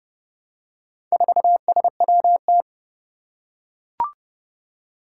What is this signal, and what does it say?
Morse code, heard as a keyed tone, about 32 wpm, 700 Hz. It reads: 4SWT
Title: Four Challenging Letter-Number Combinations 30wpm